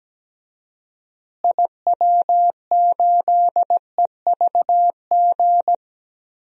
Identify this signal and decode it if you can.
Morse code, heard as a keyed tone, about 17 wpm, 700 Hz. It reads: IW8EVG
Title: DXCC-100 Call Signs - 6 Characters - Repeated 2x - 17wpm